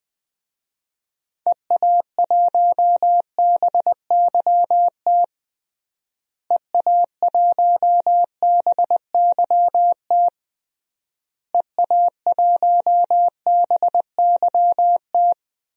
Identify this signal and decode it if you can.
Morse code, heard as a keyed tone, about 19 wpm, 700 Hz. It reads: EA1BYT EA1BYT EA1BYT
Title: DXCC-100 Call Signs - 6 Characters - Repeated 3x - 20wpm